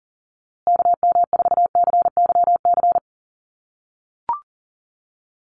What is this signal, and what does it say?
Morse code, heard as a keyed tone, about 40 wpm, 700 Hz. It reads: KM4CYC
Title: USA Call Signs Repeated 3x 40wpm